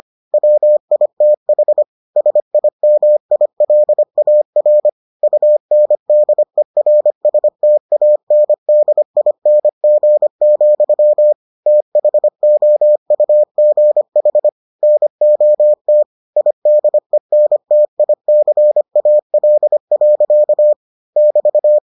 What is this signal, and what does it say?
Morse code, heard as a keyed tone, about 25 wpm, 600 Hz. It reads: WITH SIMILAR UNDERSTANDING, THOUGH NOT IDENTICAL. <BT>